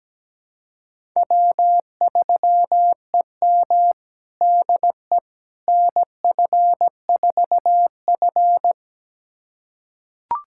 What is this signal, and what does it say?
Morse code, heard as a keyed tone, about 17 wpm, 700 Hz. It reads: W3EM DE NF4F
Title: USA Callsign DE Callsign - 4 Characters - 17wpm